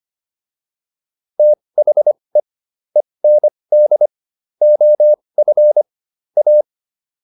Morse code, heard as a keyed tone, about 25 wpm, spaced as for 20 wpm, 600 Hz. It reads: THE END OF A